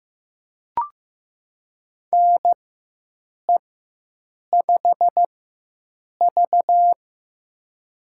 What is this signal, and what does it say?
Morse code, heard as a keyed tone, about 15 wpm, spaced as for 7 wpm, 700 Hz. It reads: NE5V
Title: USA Call Signs - 4 Characters - 4x Character Spacing - 15wpm